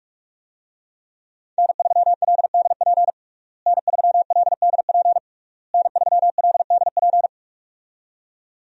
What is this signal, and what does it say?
Morse code, heard as a keyed tone, about 46 wpm, 700 Hz. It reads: N3LDP N3LDP N3LDP
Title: USA Call Signs - 5 Characters - Repeated 3x - 45wpm